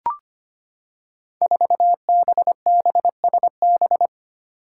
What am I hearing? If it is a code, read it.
Morse code, heard as a keyed tone, about 25 wpm, 700 Hz. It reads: 4BBSB
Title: Five Challenging Letter-Number Combinations 25wpm